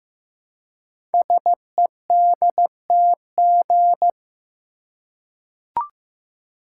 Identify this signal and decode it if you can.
Morse code, heard as a keyed tone, about 15 wpm, 700 Hz. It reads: SEDTG
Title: Code Groups - 5 Letter:Numbers - 15wpm